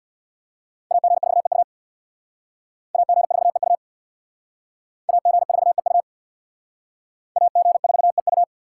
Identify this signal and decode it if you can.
Morse code, heard as a keyed tone, about 64 wpm, 700 Hz. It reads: AG4EU AG4EU AG4EU AG4EU